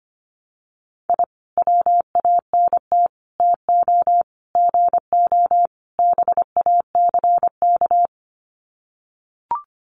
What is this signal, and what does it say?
Morse code, heard as a keyed tone, about 25 wpm, 700 Hz. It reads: I WANT TO GO BACK
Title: Sets of 5 Words Encore 25wpm